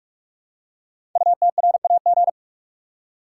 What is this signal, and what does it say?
Morse code, heard as a keyed tone, about 44 wpm, 700 Hz. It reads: UTRAG
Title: Code Groups - 5 Letter:Numbers - 45wpm